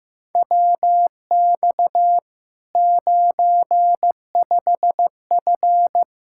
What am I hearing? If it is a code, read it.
Morse code, heard as a keyed tone, about 15 wpm, 700 Hz. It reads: WX 95F
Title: K1USN SST QSO Practice 15wpm